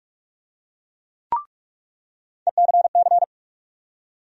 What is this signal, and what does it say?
Morse code, heard as a keyed tone, about 46 wpm, 700 Hz. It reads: ECC
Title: USA Call Sign Postfixes - 3 Letters 45wpm